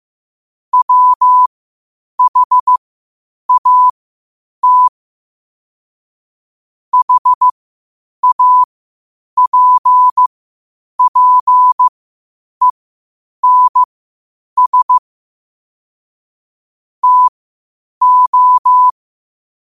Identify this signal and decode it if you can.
Morse code, heard as a keyed tone, about 15 wpm, spaced as for 8 wpm, 1000 Hz. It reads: WHAT HAPPENS TO
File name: qotdfiles / 8wpm